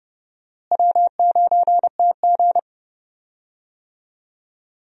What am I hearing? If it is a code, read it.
Morse code, heard as a keyed tone, about 30 wpm, 700 Hz. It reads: W9TG
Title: USA Call Signs - 4 Characters - Repeated 2x - 30wpm